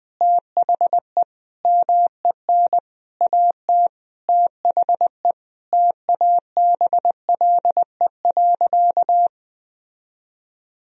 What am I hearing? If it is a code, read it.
Morse code, heard as a keyed tone, about 20 wpm, 700 Hz. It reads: THE MEN AT THE TABLE.
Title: Sentences from Top 1000 Words - Review Part 2 - 20wpm